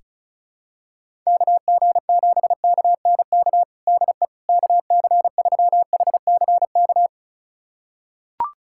Morse code, heard as a keyed tone, about 34 wpm, 700 Hz. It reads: KG7KNK DE KC3HCK
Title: USA Callsign DE Callsign - 6 Characters - 35wpm